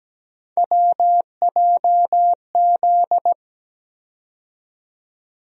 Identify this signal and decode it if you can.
Morse code, heard as a keyed tone, about 17 wpm, 700 Hz. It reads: WJZ
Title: USA Call Sign Postfixes - 3 Letters 17wpm